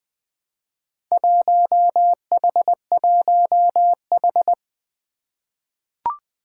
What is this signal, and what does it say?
Morse code, heard as a keyed tone, about 20 wpm, 700 Hz. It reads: 1H1H
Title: Four Challenging Letter-Number Combinations 20wpm